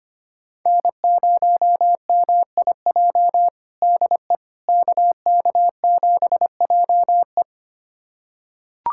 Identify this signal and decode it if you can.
Morse code, heard as a keyed tone, about 25 wpm, 700 Hz. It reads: N0MIJ DE KK7JE
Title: USA Callsign DE Callsign - 5 Characters - 25wpm